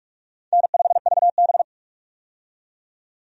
Morse code, heard as a keyed tone, about 47 wpm, 700 Hz. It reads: N5VB
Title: USA Call Signs - 4 Characters - 45wpm